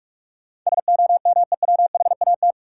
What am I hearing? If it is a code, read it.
Morse code, heard as a keyed tone, about 46 wpm, 700 Hz. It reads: SOMEWHAT
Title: Sentences from Top 1000 Words - Review Part 1 - New Word Drill - 45wpm